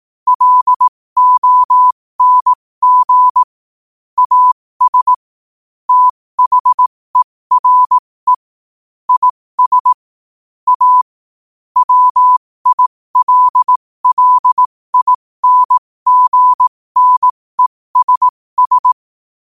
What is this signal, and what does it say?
Morse code, heard as a keyed tone, about 18 wpm, spaced as for 15 wpm, 1000 Hz. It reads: LONG AS THERE IS A WILLINGNESS
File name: qotdfiles / 15wpm